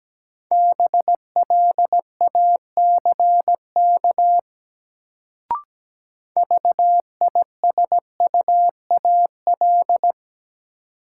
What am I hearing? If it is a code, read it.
Morse code, heard as a keyed tone, about 17 wpm, 700 Hz. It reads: BLACK VISUAL
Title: Sets of 5 Words Encore - New Word Drill - 17wpm